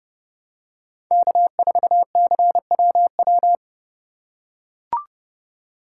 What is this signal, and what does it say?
Morse code, heard as a keyed tone, about 32 wpm, 700 Hz. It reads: K4CWW
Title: USA Call Signs - 5 Characters - 3x Character Spacing - 30wpm